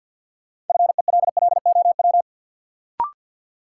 Morse code, heard as a keyed tone, about 52 wpm, 700 Hz. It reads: FELLOW
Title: Sets of 5 Words Encore - New Word Drill - 50wpm